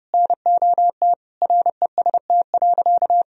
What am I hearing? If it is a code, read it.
Morse code, heard as a keyed tone, about 31 wpm, 700 Hz. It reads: NOT REST.
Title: Sentences from Top 1000 Words - Review Part 2 - 30wpm